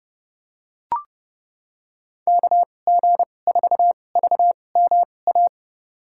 Morse code, heard as a keyed tone, about 32 wpm, spaced as for 22 wpm, 700 Hz. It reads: KG4VMA
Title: USA Call Signs - 6 Characters - 2x Character Spacing - 30wpm